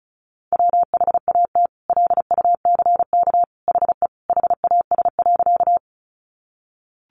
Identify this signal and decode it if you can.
Morse code, heard as a keyed tone, about 34 wpm, 700 Hz. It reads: WHAT LUCK HE HAS.
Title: Sentences from Top 1500 Words 35wpm